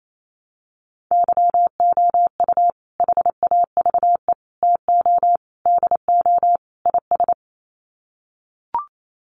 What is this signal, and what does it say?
Morse code, heard as a keyed tone, about 28 wpm, 700 Hz. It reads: YOU HAVE TO DO IS
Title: Sets of 5 Words 28wpm